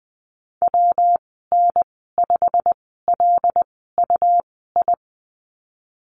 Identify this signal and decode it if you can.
Morse code, heard as a keyed tone, about 19 wpm, spaced as for 14 wpm, 700 Hz. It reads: WN5LUI
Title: 128 Lesson 21g - Number 6 - Call Signs 20wpm